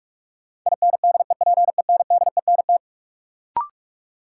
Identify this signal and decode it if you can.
Morse code, heard as a keyed tone, about 47 wpm, 700 Hz. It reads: INDEPENDENT
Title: Sentences from Top 900 Words - New Word Drill - 45wpm